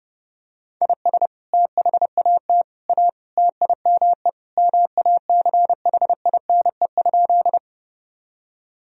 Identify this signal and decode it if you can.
Morse code, heard as a keyed tone, about 30 wpm, 700 Hz. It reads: IS THAT A TIME MACHINE?
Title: Sentences from Top 1000 Words - Review Part 2 - 30wpm